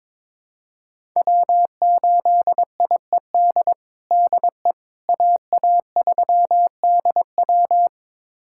Morse code, heard as a keyed tone, about 22 wpm, 700 Hz. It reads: W8IED DE AA3DW
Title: USA Callsign DE Callsign - 5 Characters - 22wpm